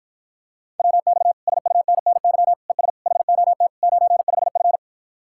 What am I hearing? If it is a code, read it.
Morse code, heard as a keyed tone, about 54 wpm, 700 Hz. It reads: WX SUNNY ES HOT 95F